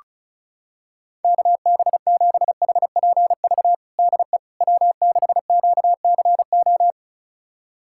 Morse code, heard as a keyed tone, about 36 wpm, 700 Hz. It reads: KB7HPV DE W6QCO
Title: USA Callsign DE Callsign 35wpm